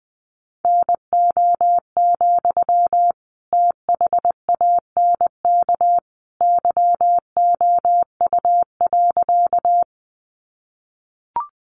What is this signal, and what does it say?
Morse code, heard as a keyed tone, about 20 wpm, 700 Hz. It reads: NO, THANK YOU.
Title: Sentences from Top 600 Words 20wpm